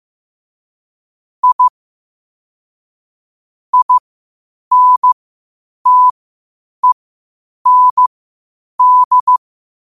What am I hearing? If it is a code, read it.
Morse code, heard as a keyed tone, about 15 wpm, spaced as for 8 wpm, 1000 Hz. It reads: I INTEND
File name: qotdfiles / 8wpm